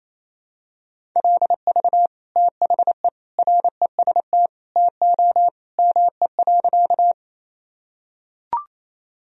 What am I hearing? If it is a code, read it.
Morse code, heard as a keyed tone, about 28 wpm, 700 Hz. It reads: LV THE REST TO ME.
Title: Sentences from Top 500 Words and Abbreviations 28wpm